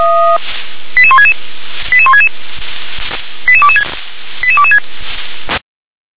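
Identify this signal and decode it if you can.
Five-tone selective call (ZVEI), heard as quick runs of five tones, tones of about 70 ms. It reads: 88166 88166 88226 88226
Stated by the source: ZVEI five-tone selective call received on an SDR.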